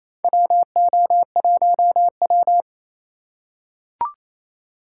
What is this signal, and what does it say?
Morse code, heard as a keyed tone, about 27 wpm, 700 Hz. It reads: WO1W
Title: USA Call Signs - 4 Characters - 6x Character Spacing - 28wpm